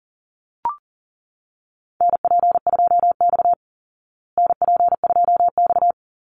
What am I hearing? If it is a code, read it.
Morse code, heard as a keyed tone, about 40 wpm, 700 Hz. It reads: NP2X NP2X
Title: Call Signs Repeated 2x - 40wpm